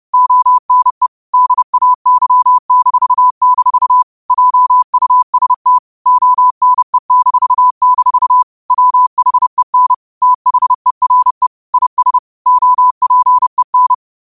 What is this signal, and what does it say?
Morse code, heard as a keyed tone, about 30 wpm, 1000 Hz. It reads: ONE DAY-- JUST ONE-- WHEN THERE IS OPEN
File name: qotdfiles / 30wpm